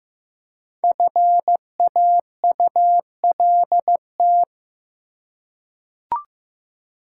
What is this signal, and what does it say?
Morse code, heard as a keyed tone, about 15 wpm, 700 Hz. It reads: FAULT
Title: Five Letter Words - 15wpm